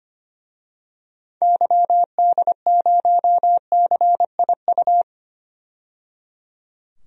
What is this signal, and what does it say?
Morse code, heard as a keyed tone, about 25 wpm, 700 Hz. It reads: YD0CIU
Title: DXCC-100 Call Signs - 6 Characters - 25wpm